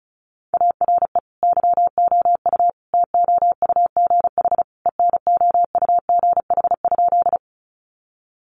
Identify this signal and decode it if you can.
Morse code, heard as a keyed tone, about 34 wpm, 700 Hz. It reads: ARE YOU TOUGH ENOUGH?
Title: Sentences from Top 1200 Words 35wpm